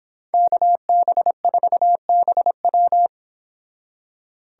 Morse code, heard as a keyed tone, about 30 wpm, 700 Hz. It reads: KB4BW